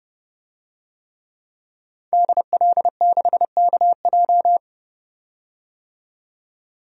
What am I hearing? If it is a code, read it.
Morse code, heard as a keyed tone, about 29 wpm, 700 Hz. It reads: DL6KJ
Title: DXCC-100 Call Signs - 5 Characters - 30wpm